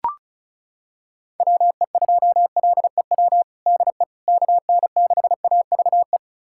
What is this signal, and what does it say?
Morse code, heard as a keyed tone, about 34 wpm, 690 Hz. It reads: WE2LEW DE KN6AVE
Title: USA Callsign DE Callsign - 6 Characters - 35wpm